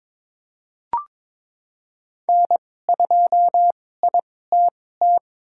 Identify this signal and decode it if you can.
Morse code, heard as a keyed tone, about 23 wpm, spaced as for 16 wpm, 700 Hz. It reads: N2ITT
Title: USA Call Signs - 5 Characters - 2x Character Spacing - 22wpm